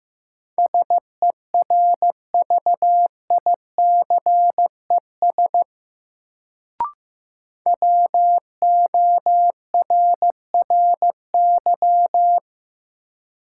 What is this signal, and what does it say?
Morse code, heard as a keyed tone, about 15 wpm, 700 Hz. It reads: SERVICES WORRY
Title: Sets of 5 Words - New Word Drill - 15wpm